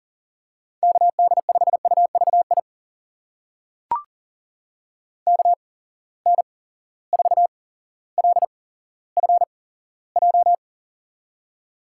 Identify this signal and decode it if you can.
Morse code, heard as a keyed tone, about 40 wpm, spaced as for 17 wpm, 700 Hz. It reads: KD5UVI KN4LFJ
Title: USA Call Signs - 6 Characters - 8x Character Spacing - 40wpm